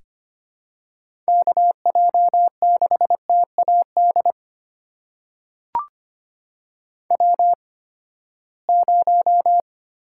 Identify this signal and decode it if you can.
Morse code, heard as a keyed tone, about 25 wpm, spaced as for 17 wpm, 700 Hz. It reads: KJ6TAD W0
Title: USA Call Signs - 8x Character Spacing - 25wpm